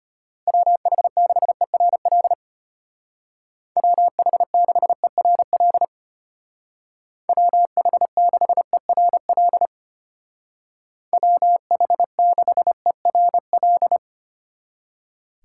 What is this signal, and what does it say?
Morse code, heard as a keyed tone, about 30 wpm, 700 Hz. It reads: WH6ERL WH6ERL WH6ERL WH6ERL
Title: Speed Racing - USA 2x3 Call Signs 25wpm